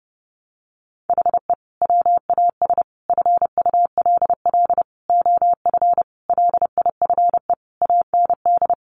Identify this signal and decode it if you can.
Morse code, heard as a keyed tone, about 30 wpm, 700 Hz. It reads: HE WAS FULL OF LIFE AND
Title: Sentences from Top 1100 Words 30wpm